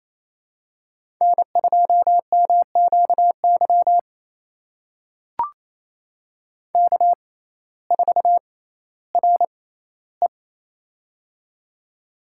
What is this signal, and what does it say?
Morse code, heard as a keyed tone, about 28 wpm, spaced as for 15 wpm, 700 Hz. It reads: N2MQY K4RE
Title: USA Call Signs - 6x Character Spacing - 28wpm